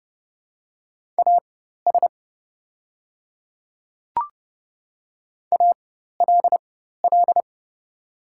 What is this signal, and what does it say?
Morse code, heard as a keyed tone, about 29 wpm, spaced as for 14 wpm, 700 Hz. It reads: AS ALL
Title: 4x Character Spacing - Top 100 Words - 30wpm